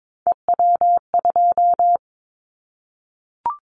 Morse code, heard as a keyed tone, about 22 wpm, 700 Hz. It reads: EW2
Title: Callsign Prefixes for Top 100 DXCC - 22wpm